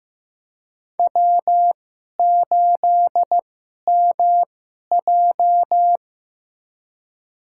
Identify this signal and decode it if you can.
Morse code, heard as a keyed tone, about 15 wpm, spaced as for 11 wpm, 700 Hz. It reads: W8MJ
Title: USA Call Signs - 4 Characters - 2x Character Spacing - 15wpm